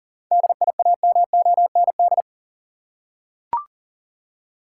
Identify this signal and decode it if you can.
Morse code, heard as a keyed tone, about 40 wpm, 700 Hz. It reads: DIAMOND
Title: Seven Letter Words - 40wpm